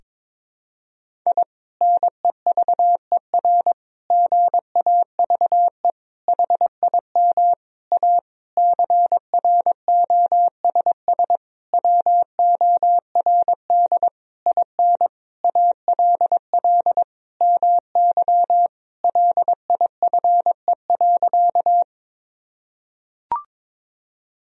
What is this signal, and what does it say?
Morse code, heard as a keyed tone, about 22 wpm, 700 Hz. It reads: I NEVER GAVE HIM A CROSS WORD IN ALL MY LIFE.
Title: Sentences from Top 1000 Words - Review Part 1 - 22wpm